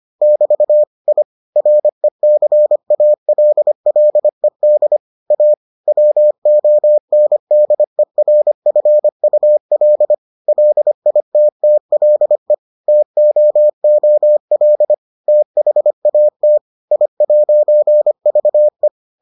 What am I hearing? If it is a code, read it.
Morse code, heard as a keyed tone, about 25 wpm, 600 Hz. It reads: <BT> I RECALLED A WONDERFUL LITTLE TOOL THAT I'VE